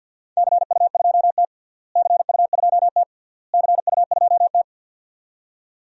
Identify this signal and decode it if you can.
Morse code, heard as a keyed tone, about 50 wpm, 700 Hz. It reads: CU2T CU2T CU2T
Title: DXCC-100 Call Signs - 4 Characters - Repeated 3x - 50wpm